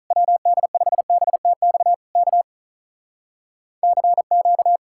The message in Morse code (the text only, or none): WD5BTX K CQ